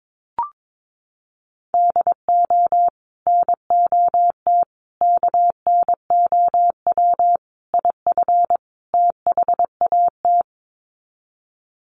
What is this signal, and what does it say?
Morse code, heard as a keyed tone, about 22 wpm, 700 Hz. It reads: DO NOT KNOW IF THAT
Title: Sets of 5 Words Encore2 22wpm